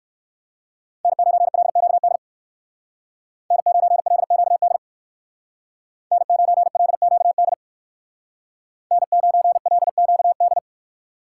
Morse code, heard as a keyed tone, about 58 wpm, 700 Hz. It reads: N9LQD N9LQD N9LQD N9LQD